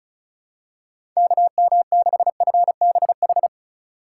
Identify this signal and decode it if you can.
Morse code, heard as a keyed tone, about 36 wpm, 700 Hz. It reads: KM6FBH
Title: USA Call Signs - 6 Characters - 4x Character Spacing - 35wpm